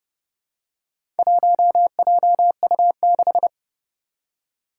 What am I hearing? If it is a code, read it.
Morse code, heard as a keyed tone, about 31 wpm, 700 Hz. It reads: 1JU6
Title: Four Challenging Letter-Number Combinations 30wpm